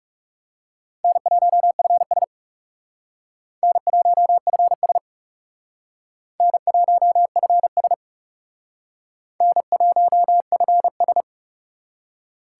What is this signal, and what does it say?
Morse code, heard as a keyed tone, about 36 wpm, 700 Hz. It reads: N1FS N1FS N1FS N1FS
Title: Speed Racing - USA 1x2 Call Signs 30wpm